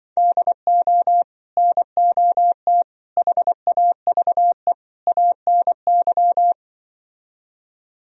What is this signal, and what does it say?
Morse code, heard as a keyed tone, about 24 wpm, 700 Hz. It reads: DO NOT HAVE ANY
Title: Speed-Racing - Sets of 4 Words 20wpm